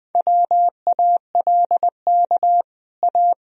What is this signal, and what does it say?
Morse code, heard as a keyed tone, about 21 wpm, 700 Hz. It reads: WALK A
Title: Sentences from Top 1000 Words - Review Part 1 - 20wpm